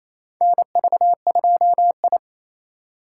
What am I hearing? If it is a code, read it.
Morse code, heard as a keyed tone, about 28 wpm, 700 Hz. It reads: NV2I
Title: USA Call Signs - 3x Character Spacing - 28wpm